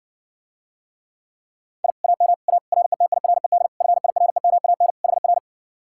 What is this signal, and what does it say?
Morse code, heard as a keyed tone, about 60 wpm, 700 Hz. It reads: I AM A RETIRED FIREMAN HR